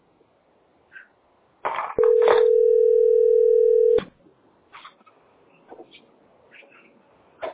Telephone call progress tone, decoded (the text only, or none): ringback tone